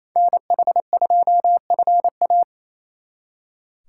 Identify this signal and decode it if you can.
Morse code, heard as a keyed tone, about 29 wpm, 700 Hz. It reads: NH2FA
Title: Speed Racing - USA Call Signs 28wpm